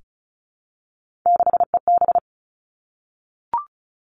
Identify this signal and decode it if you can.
Morse code, heard as a keyed tone, about 36 wpm, 700 Hz. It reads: 6EB
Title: Three Challenging Letter-Number Combinations 35wpm